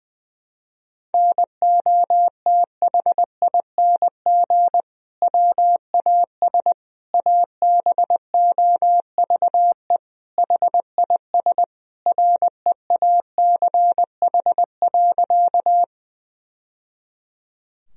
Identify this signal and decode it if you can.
Morse code, heard as a keyed tone, about 20 wpm, 700 Hz. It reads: NOTHING WAS ABOVE HIS REACH.
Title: Sentences from Top 1000 Words - Review Part 2 - 20wpm